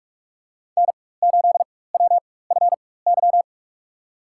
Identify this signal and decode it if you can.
Morse code, heard as a keyed tone, about 46 wpm, spaced as for 21 wpm, 700 Hz. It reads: N8WFY
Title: USA Call Signs - 5 Characters - 4x Character Spacing - 45wpm